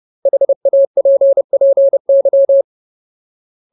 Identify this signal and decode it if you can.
Morse code, heard as a keyed tone, about 30 wpm, 550 Hz. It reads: HAPPY
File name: Ultimate Word List / 30WPM Ultimate